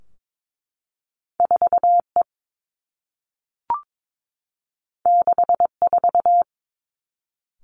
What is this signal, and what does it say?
Morse code, heard as a keyed tone, about 22 wpm, 700 Hz. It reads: 4 E 6 4
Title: Two Challenging Letter-Number Combinations 22wpm